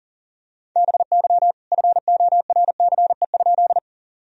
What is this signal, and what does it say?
Morse code, heard as a keyed tone, about 40 wpm, 700 Hz. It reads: BY FORCE?
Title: Sentences from Top 1000 Words - Review Part 3 - 40wpm